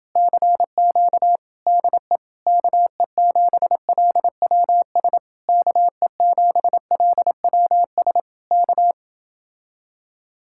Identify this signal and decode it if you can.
Morse code, heard as a keyed tone, about 30 wpm, 700 Hz. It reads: CQ DE KE7LWS KE7LWS K